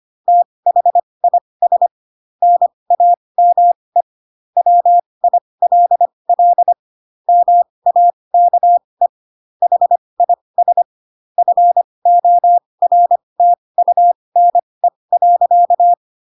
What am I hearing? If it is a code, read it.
Morse code, heard as a keyed tone, about 25 wpm, spaced as for 20 wpm, 700 Hz. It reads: THIS NAME WILL MAKE HIS FORTUNE.
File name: Pinocchio / Pinocchio ch3 15 25 ik1hge